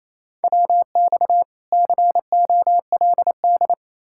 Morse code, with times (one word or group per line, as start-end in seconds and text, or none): 0.4-1.4 WX
1.7-3.7 COLD